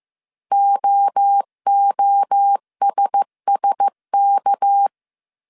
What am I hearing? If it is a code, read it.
Morse code, heard as a keyed tone, about 15 wpm, 800 Hz. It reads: OOSSK
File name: CW / DET / 1 9 1500C S C 14wpm 3245